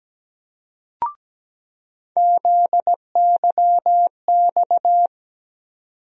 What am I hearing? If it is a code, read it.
Morse code, heard as a keyed tone, about 17 wpm, 700 Hz. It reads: ZYX
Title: USA Call Sign Postfixes - 3 Letters 17wpm